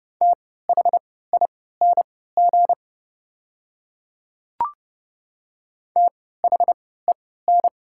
Morse code, heard as a keyed tone, about 30 wpm, spaced as for 17 wpm, 700 Hz. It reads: THING THEN
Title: 3x Character Spacing - Top 300 Words - 30wpm